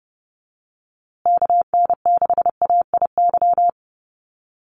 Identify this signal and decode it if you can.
Morse code, heard as a keyed tone, about 30 wpm, 700 Hz. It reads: KN6AIY
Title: USA Call Signs Repeated 2x 30wpm